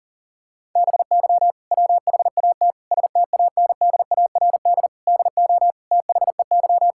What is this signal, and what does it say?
Morse code, heard as a keyed tone, about 40 wpm, 700 Hz. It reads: BY WHAT STANDARD DO THEY
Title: Sentences from Top 1000 Words - Review Part 6 - 40wpm